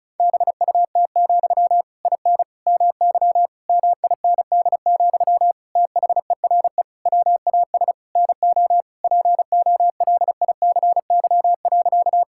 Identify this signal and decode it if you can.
Morse code, heard as a keyed tone, about 34 wpm, 700 Hz. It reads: BUT, IN MY MIND, THERE WAS NO POLICY.
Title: Sentences from Top 1000 Words - Review Part 5 - 35wpm